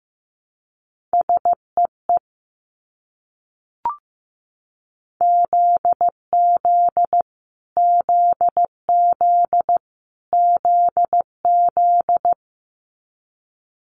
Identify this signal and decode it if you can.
Morse code, heard as a keyed tone, about 15 wpm, 700 Hz. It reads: SEE ZZ ZZ ZZ
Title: Multi-letter Phonemes Repeated 3x 15wpm